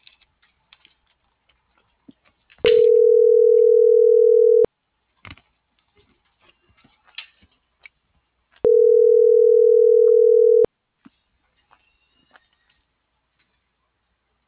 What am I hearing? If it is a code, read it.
Ringback tone (North America), 440 + 480 Hz.